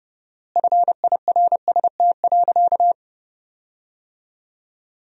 Morse code, heard as a keyed tone, about 30 wpm, 700 Hz. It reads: FIRST.
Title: 4x Word-Spacing - Sentences from Top 400 Words - 30wpm